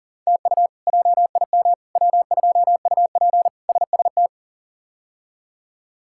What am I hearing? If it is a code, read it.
Morse code, heard as a keyed tone, about 40 wpm, 700 Hz. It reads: TU JIM W2UP SST
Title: K1USN SST QSO Practice - 40wpm